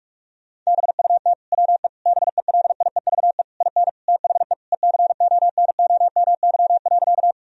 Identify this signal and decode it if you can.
Morse code, heard as a keyed tone, about 47 wpm, 700 Hz. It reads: BUT WE BELIEVE IN THE ECONOMY.
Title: Sentences from Top 1000 Words - Review Part 3 - 45wpm